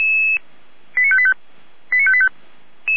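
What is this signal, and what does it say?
Five-tone selective call (ZVEI), heard as quick runs of five tones, tones of about 70 ms. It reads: 89585 89585
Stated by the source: ZVEI five-tone call 89585 received by SDR.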